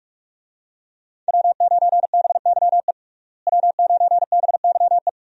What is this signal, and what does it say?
Morse code, heard as a keyed tone, about 46 wpm, 700 Hz. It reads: W9BYE W9BYE
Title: USA Call Signs - 5 Characters - Repeated 3x - 45wpm